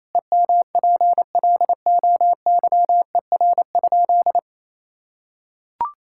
Morse code, heard as a keyed tone, about 28 wpm, 700 Hz. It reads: EMPLOYER?
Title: Sentences from Top 1600 Words 28wpm